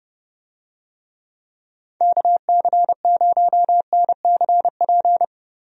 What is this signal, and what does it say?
Morse code, heard as a keyed tone, about 29 wpm, 700 Hz. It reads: KC0NCP
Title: USA Call Signs - 6 Characters - Repeated 3x - 30wpm